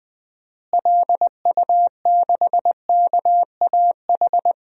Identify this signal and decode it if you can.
Morse code, heard as a keyed tone, about 20 wpm, 700 Hz. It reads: LU6KAH
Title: DXCC-100 Call Signs - 6 Characters - Repeated 2x - 20wpm